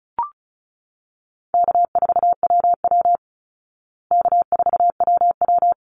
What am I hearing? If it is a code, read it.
Morse code, heard as a keyed tone, about 34 wpm, 700 Hz. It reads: K4WW K4WW
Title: Easy Call Signs Repeated 3x - 35wpm